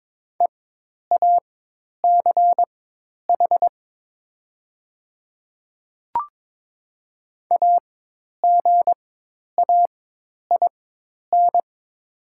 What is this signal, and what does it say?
Morse code, heard as a keyed tone, about 22 wpm, spaced as for 10 wpm, 700 Hz. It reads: EACH AGAIN
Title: 4x Character Spacing - Top 200 Words - 22wpm